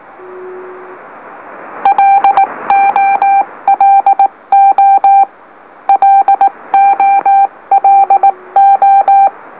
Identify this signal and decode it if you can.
Morse code, heard as a keyed tone, about 18 wpm, 790 Hz. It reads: LOLO LOLO
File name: M04- -2345utc-LOLO